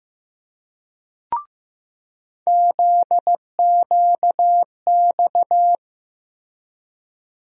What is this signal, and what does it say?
Morse code, heard as a keyed tone, about 16 wpm, 700 Hz. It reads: ZQX
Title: USA Call Sign Postfixes - 3 Letters 15wpm